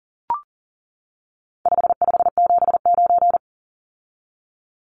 Morse code, heard as a keyed tone, about 40 wpm, 700 Hz. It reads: 5579
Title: Numbers - 4-Digit 40wpm